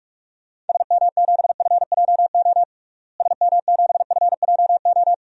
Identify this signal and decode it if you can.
Morse code, heard as a keyed tone, about 46 wpm, 700 Hz. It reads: SM7FJO SM7FJO
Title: DXCC-100 Call Signs - 6 Characters - Repeated 2x - 45wpm